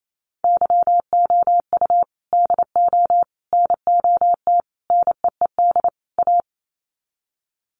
Morse code, heard as a keyed tone, about 28 wpm, 690 Hz. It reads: YOU DO NOT NEED A
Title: Sets of 5 Words Encore 28wpm